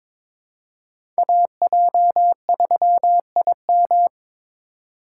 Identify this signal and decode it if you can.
Morse code, heard as a keyed tone, about 23 wpm, 700 Hz. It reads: AJ3IM
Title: USA Call Signs - 8x Character Spacing - 22wpm